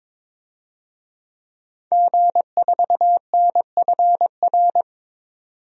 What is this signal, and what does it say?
Morse code, heard as a keyed tone, about 22 wpm, 700 Hz. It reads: G4NFR
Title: DXCC-100 Call Signs - 5 Characters - 22wpm